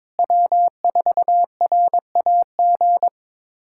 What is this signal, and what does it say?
Morse code, heard as a keyed tone, about 22 wpm, 700 Hz. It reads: W4RAG
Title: USA Callsign DE Callsign - 5 Characters - 22wpm